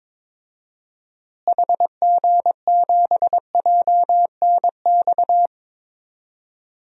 Morse code, heard as a keyed tone, about 22 wpm, 700 Hz. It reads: HG7JNX
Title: DXCC-100 Call Signs - 6 Characters - Repeated 3x - 22wpm